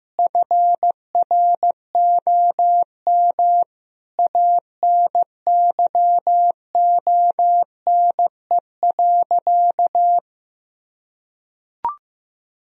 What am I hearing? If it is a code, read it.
Morse code, heard as a keyed tone, about 14 wpm, 700 Hz. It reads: FROM ANYONE.
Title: Sentences from Top 1000 Words - Review Part 3 - 15wpm